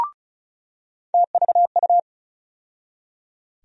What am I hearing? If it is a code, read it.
Morse code, heard as a keyed tone, about 36 wpm, 700 Hz. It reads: TVU
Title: USA Call Sign Postfixes - 3 Letters 35wpm